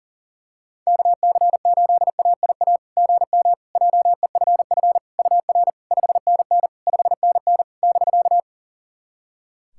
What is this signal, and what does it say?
Morse code, heard as a keyed tone, about 40 wpm, 700 Hz. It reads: KC8AIA GM JEFF UR 5NN 5NN <BK>
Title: SOTA QSO Practice 40wpm